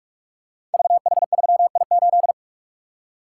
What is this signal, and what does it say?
Morse code, heard as a keyed tone, about 44 wpm, 700 Hz. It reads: VH3I8
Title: Code Groups - 5 Letter:Numbers - 45wpm